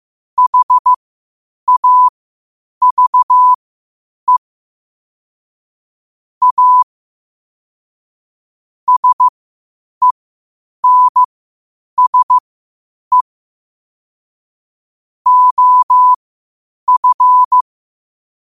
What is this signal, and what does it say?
Morse code, heard as a keyed tone, about 15 wpm, spaced as for 8 wpm, 1000 Hz. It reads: HAVE A SENSE OF
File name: qotdfiles / 8wpm